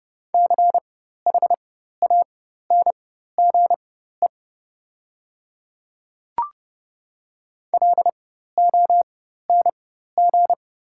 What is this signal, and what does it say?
Morse code, heard as a keyed tone, about 30 wpm, spaced as for 14 wpm, 700 Hz. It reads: CHANGE LONG
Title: 4x Character Spacing - Top 200 Words - 30wpm